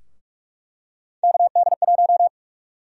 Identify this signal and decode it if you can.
Morse code, heard as a keyed tone, about 46 wpm, 700 Hz. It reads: KD1
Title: USA Call Sign Prefixes - Repeated 2x - 45wpm